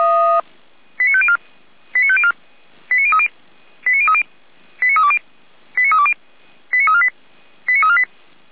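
Five-tone selective call (ZVEI), heard as quick runs of five tones, tones of about 70 ms. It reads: 89504 89504 89030 89030 89320 89320 89348 89348
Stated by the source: ZVEI five-tone selective call 89050 received on SDR.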